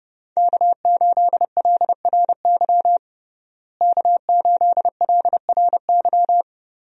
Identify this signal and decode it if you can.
Morse code, heard as a keyed tone, about 30 wpm, 700 Hz. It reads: K8LRY K8LRY
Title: USA Call Signs - 5 Characters - Repeated 3x - 30wpm